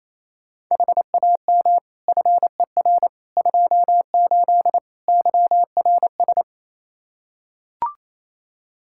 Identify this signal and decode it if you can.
Morse code, heard as a keyed tone, about 27 wpm, 700 Hz. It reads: HAM FER 28 YRS
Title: Practice QSO - Exchange 3 - More Info - 28wpm